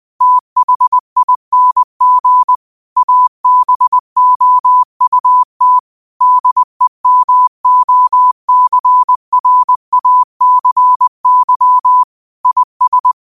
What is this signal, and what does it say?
Morse code, heard as a keyed tone, about 20 wpm, 1000 Hz. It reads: THING ABOUT DEMOCRACY IS
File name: qotdfiles / 20wpm